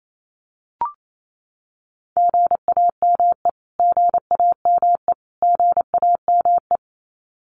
Morse code, heard as a keyed tone, about 28 wpm, 700 Hz. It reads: GAME GAME GAME
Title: Top 500 Words Repeated 28wpm